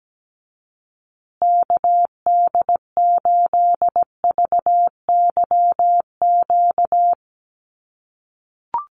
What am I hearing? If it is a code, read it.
Morse code, heard as a keyed tone, about 17 wpm, 700 Hz. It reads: KD8VYQ
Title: USA Call Signs Repeated 2x 17wpm